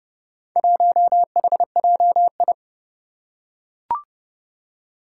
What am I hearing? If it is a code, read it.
Morse code, heard as a keyed tone, about 28 wpm, 700 Hz. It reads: 1HJI
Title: Four Challenging Letter-Number Combinations 30wpm